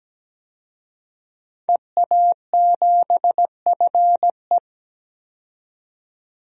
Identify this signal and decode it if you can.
Morse code, heard as a keyed tone, about 17 wpm, 700 Hz. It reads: EA7FE
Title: DXCC-100 Call Signs - 5 Characters - 17wpm